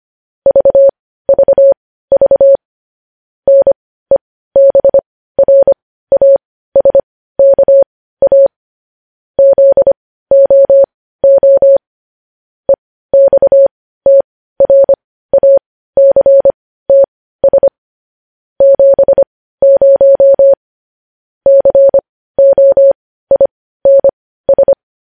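Morse code, heard as a keyed tone, about 25 wpm, spaced as for 15 wpm, 550 Hz. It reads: VVV NEBRASKA ZOO EXTRACTS 70 COINS
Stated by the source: Morse code news headlines sent at 15, opening with vvv.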